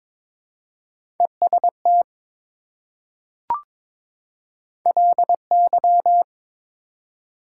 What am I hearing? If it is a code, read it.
Morse code, heard as a keyed tone, about 21 wpm, 700 Hz. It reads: EST LY
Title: Prefixes and Suffixes 22wpm